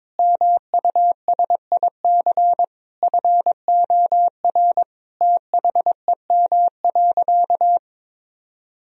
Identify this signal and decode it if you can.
Morse code, heard as a keyed tone, about 22 wpm, 700 Hz. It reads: MUSIC FOR THEM.
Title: Sentences from Top 1000 Words - Review Part 1 - 22wpm